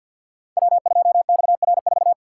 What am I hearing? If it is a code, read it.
Morse code, heard as a keyed tone, about 47 wpm, 700 Hz. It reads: W2XR4
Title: Code Groups - 5 Letter:Numbers - 50wpm